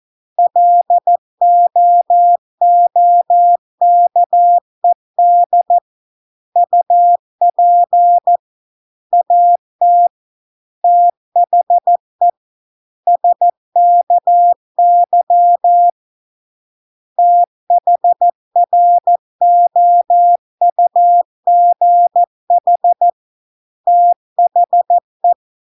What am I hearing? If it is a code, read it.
Morse code, heard as a keyed tone, about 14 wpm, 700 Hz. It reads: LOOKED UP AT THE SKY THROUGH THE